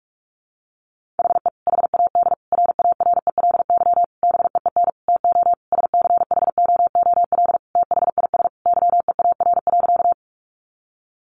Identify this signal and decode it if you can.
Morse code, heard as a keyed tone, about 44 wpm, 700 Hz. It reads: HE HAD RARELY BEEN TO SCHOOL THIS YEAR.
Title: Sentences from Top 1400 Words 45wpm